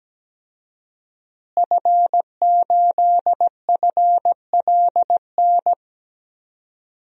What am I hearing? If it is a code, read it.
Morse code, heard as a keyed tone, about 17 wpm, 700 Hz. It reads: F8FLN
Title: DXCC-100 Call Signs - 5 Characters - Repeated 2x - 17wpm